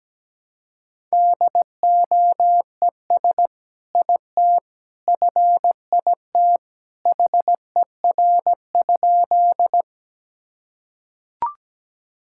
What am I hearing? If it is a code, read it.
Morse code, heard as a keyed tone, about 17 wpm, 700 Hz. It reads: DOES IT FIT HER?
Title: Sentences from Top 1000 Words - Review Part 5 - 17wpm